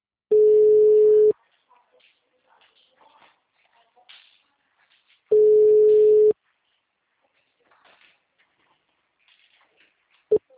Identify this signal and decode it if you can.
Ringback tone (Europe, 425 Hz), with speech.